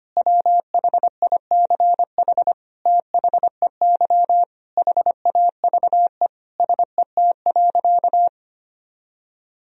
Morse code, heard as a keyed tone, about 25 wpm, 700 Hz. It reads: WHICH THEY HAVE SET.
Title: Sentences from Top 1000 Words - Review Part 2 - 25wpm